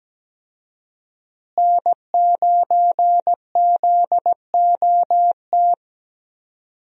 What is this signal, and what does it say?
Morse code, heard as a keyed tone, about 17 wpm, 700 Hz. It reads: N9ZOT
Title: USA Call Signs - 5 Characters - Repeated 2x - 17wpm